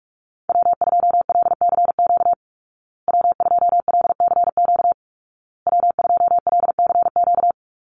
Morse code, heard as a keyed tone, about 44 wpm, 700 Hz. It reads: W2LCQ W2LCQ W2LCQ
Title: Call Signs Repeated 2x - 45wpm